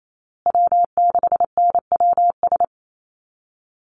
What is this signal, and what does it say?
Morse code, heard as a keyed tone, about 28 wpm, 700 Hz. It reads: W6NWS
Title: Call Signs Repeated 3x - 28wpm